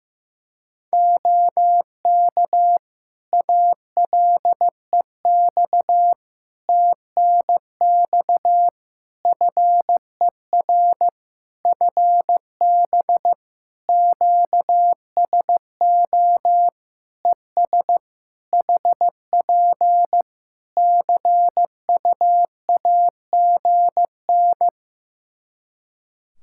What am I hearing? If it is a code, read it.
Morse code, heard as a keyed tone, about 15 wpm, 700 Hz. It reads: OK ALEX TNX FER FB QSO ES HP CUAGN
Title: Practice QSO - Exchange 4 - Ending - 15wpm